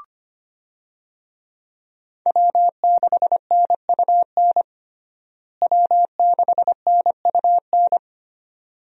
Morse code, heard as a keyed tone, about 25 wpm, 700 Hz. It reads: W6NUN W6NUN
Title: USA Call Signs - 5 Characters - Repeated 2x - 25wpm